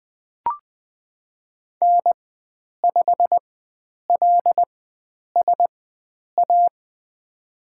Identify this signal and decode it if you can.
Morse code, heard as a keyed tone, about 20 wpm, spaced as for 9 wpm, 700 Hz. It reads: N5LSA
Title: USA Call Signs - 5 Characters - 4x Character Spacing - 20wpm